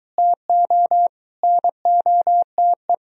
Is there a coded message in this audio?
Yes: Morse code from 0.2 to 2.9 s, heard as a keyed tone, about 22 wpm, 700 Hz. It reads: TO NOTE